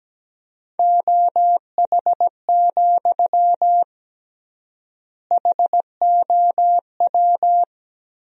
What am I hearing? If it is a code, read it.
Morse code, heard as a keyed tone, about 17 wpm, 700 Hz. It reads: OH, HOW
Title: 3x Word-Spacing - Sentences from Top 500 Words - 17wpm